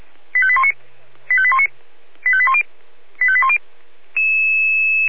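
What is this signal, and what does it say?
Five-tone selective call (ZVEI), heard as quick runs of five tones, tones of about 70 ms. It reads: 85819 85819 85810 85810
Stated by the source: ZVEI five-tone selective call 85855, received by SDR.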